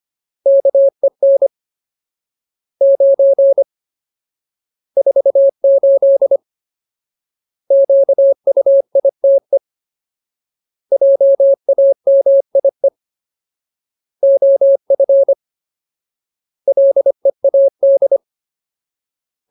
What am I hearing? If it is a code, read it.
Morse code, heard as a keyed tone, about 25 wpm, 550 Hz. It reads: KEN 9 48 QUITE JAMIE OF LEAD
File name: Ultimate Word List / 25WPM Ultimate